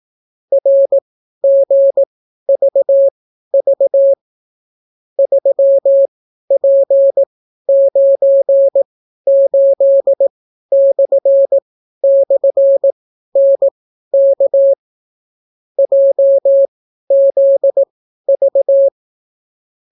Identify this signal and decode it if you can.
Morse code, heard as a keyed tone, about 18 wpm, spaced as for 12 wpm, 550 Hz. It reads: RGVV 3P98//NK JZV